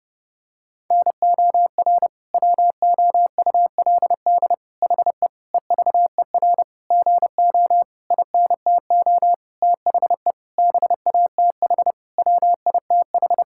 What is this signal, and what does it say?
Morse code, heard as a keyed tone, about 32 wpm, 700 Hz. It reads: NOR WOULD HE EVER GO INTO THE BATH WITH
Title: Sentences from Top 1600 Words 30wpm